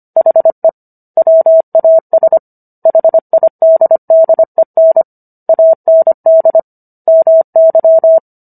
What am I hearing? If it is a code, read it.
Morse code, heard as a keyed tone, about 25 wpm, 650 Hz. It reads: HE WAS HIDDEN AND MY